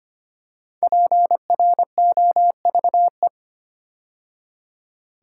Morse code, heard as a keyed tone, about 25 wpm, 700 Hz. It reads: PROVE
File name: 5 Letter Words - First 3 Letters Given - 25wpm